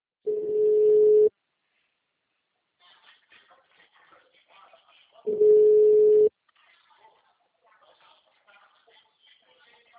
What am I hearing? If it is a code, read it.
Ringback tone (Europe, 425 Hz), with speech.
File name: out- -2006- - - .60906